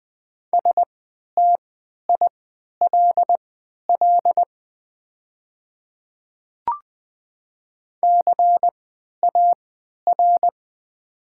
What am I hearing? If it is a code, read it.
Morse code, heard as a keyed tone, about 20 wpm, spaced as for 11 wpm, 700 Hz. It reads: STILL CAR
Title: 3x Character Spacing - Top 300 Words - 20wpm